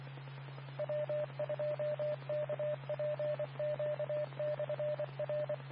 Morse code, heard as a keyed tone, about 25 wpm, 600 Hz, with radio noise. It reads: W2KPQ/R
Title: Nassau and Suffolk County Amateur Repeaters